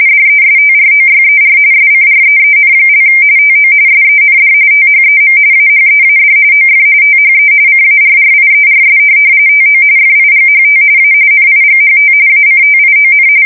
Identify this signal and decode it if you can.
RTTY, about 45.45 baud. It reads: DESTROYER SCREEN) TO SEND THEM AHEAD OF THE CARRIERS TO FIGHT A NIGHT ACTION
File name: fsk170 / USS-Massachusetts